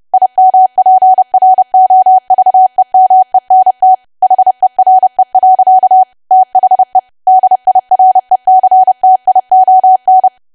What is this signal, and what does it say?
Morse code, heard as a keyed tone, about 30 wpm, 750 Hz. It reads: IMPROVEMENT HERE. THE DIRECTION
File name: cwa / W1AW-Level3 / 30WPM